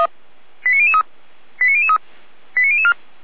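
Five-tone selective call (ZVEI), heard as quick runs of five tones, tones of about 70 ms. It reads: 89003 89003 89004
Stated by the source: ZVEI five-tone call received by SDR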